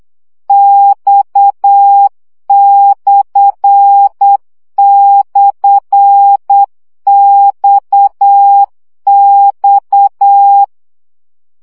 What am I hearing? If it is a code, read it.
Morse code, heard as a keyed tone, about 8 wpm, 800 Hz. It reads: X//XX
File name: CW gyakorlas x per 7 baud